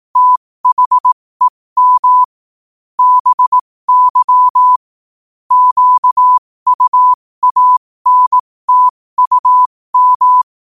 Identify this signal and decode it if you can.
Morse code, heard as a keyed tone, about 18 wpm, spaced as for 15 wpm, 1000 Hz. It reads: THEM BY QUANTUM
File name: qotdfiles / 15wpm